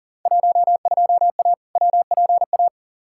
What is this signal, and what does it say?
Morse code, heard as a keyed tone, about 40 wpm, 700 Hz. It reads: 12A WPA
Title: Field Day Exchanges Encore - 40wpm